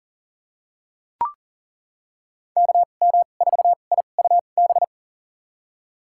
Morse code, heard as a keyed tone, about 40 wpm, spaced as for 29 wpm, 700 Hz. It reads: KM4IUB
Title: USA Call Signs - 6 Characters - 2x Character Spacing - 40wpm